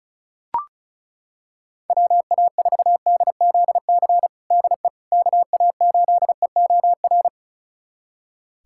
Morse code, heard as a keyed tone, about 34 wpm, 700 Hz. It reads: WA4DZC DE KA8EOR
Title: USA Callsign DE Callsign - 6 Characters - 35wpm